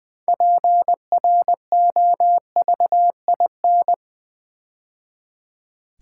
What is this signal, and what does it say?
Morse code, heard as a keyed tone, about 20 wpm, 700 Hz. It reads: PROVIN
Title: Build A Word - 2 Letters At A Time - 8-Letter Words - 20wpm